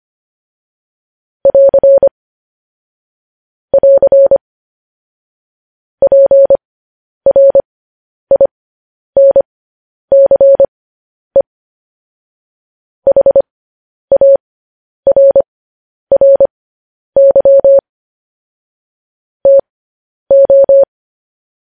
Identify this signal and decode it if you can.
Morse code, heard as a keyed tone, about 25 wpm, spaced as for 10 wpm, 550 Hz. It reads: <AR> <AR> PRINCE HARRY TO